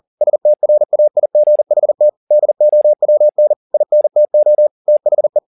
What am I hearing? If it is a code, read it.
Morse code, heard as a keyed tone, about 40 wpm, 600 Hz. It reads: STRAIGHT DOWN INTO THE